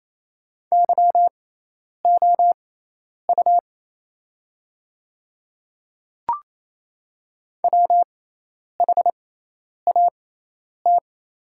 Morse code, heard as a keyed tone, about 28 wpm, spaced as for 10 wpm, 690 Hz. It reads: YOU WHAT
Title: 6x Character Spacing - Top 100 Words - 28wpm